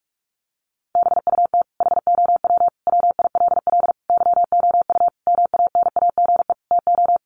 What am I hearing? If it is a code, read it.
Morse code, heard as a keyed tone, about 44 wpm, 700 Hz. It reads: BUT HOW WILL YOU MANAGE TO
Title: Sentences from Top 1400 Words 45wpm